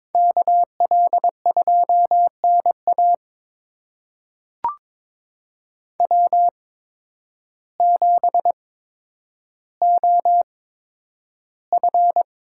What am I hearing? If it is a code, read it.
Morse code, heard as a keyed tone, about 21 wpm, spaced as for 10 wpm, 700 Hz. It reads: KL2NA W7OF
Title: USA Call Signs - 8x Character Spacing - 22wpm